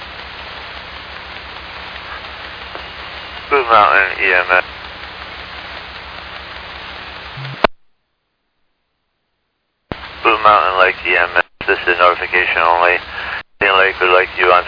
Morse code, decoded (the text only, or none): none